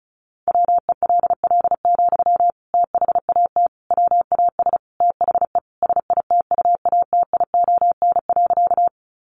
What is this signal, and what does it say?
Morse code, heard as a keyed tone, about 34 wpm, 700 Hz. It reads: WELL, THAT WAS THE SITUATION.
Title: Sentences from Top 1200 Words 35wpm